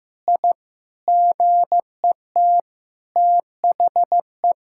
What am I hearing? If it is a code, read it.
Morse code, heard as a keyed tone, about 15 wpm, 700 Hz. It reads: I GET THE
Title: Sentences from Top 1000 Words - Review Part 5 - 15wpm